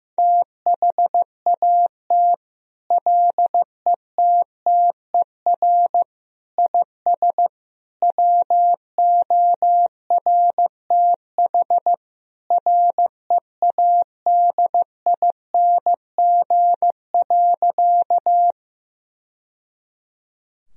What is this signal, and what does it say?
Morse code, heard as a keyed tone, about 15 wpm, 700 Hz. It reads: THAT LETTER IS WORTH READING.
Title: Sentences from Top 1000 Words - Review Part 3 - 15wpm